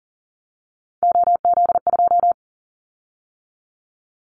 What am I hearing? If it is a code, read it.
Morse code, heard as a keyed tone, about 41 wpm, 700 Hz. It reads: OZ2
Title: Callsign Prefixes for Top 50 DXCC - 40wpm